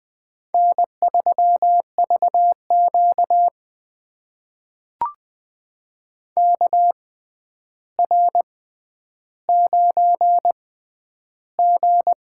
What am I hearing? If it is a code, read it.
Morse code, heard as a keyed tone, about 20 wpm, spaced as for 10 wpm, 700 Hz. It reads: N3VQ KR9G
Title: USA Call Signs - 4 Characters - 6x Character Spacing - 20wpm